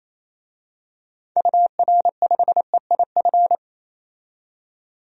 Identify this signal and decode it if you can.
Morse code, heard as a keyed tone, about 28 wpm, 700 Hz. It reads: UR5EIF
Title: DXCC-100 Call Signs - 6 Characters - 28wpm